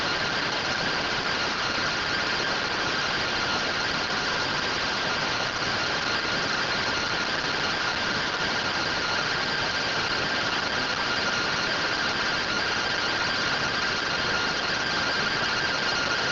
Two-tone teleprinter signal, about 170 Hz shift.
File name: findings / MR AXIAL 33,99mm 1.gen-05 rtty rx